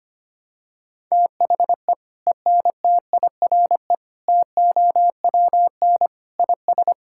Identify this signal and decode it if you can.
Morse code, heard as a keyed tone, about 25 wpm, 700 Hz. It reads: THE ENTIRE TOWN IS
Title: Sentences from Top 1000 Words - Review Part 5 - 25wpm